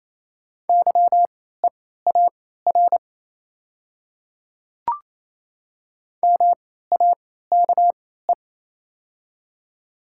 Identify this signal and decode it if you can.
Morse code, heard as a keyed tone, about 28 wpm, spaced as for 16 wpm, 690 Hz. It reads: YEAR MAKE
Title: 3x Character Spacing - Top 500 Words - 28wpm